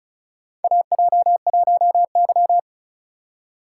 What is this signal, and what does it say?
Morse code, heard as a keyed tone, about 36 wpm, 700 Hz. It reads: AJ1Y
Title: USA Call Signs - 4 Characters - 3x Character Spacing - 35wpm